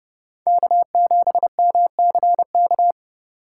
Morse code, heard as a keyed tone, about 32 wpm, 700 Hz. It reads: K7MCK
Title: USA Call Signs - 5 Characters - 3x Character Spacing - 30wpm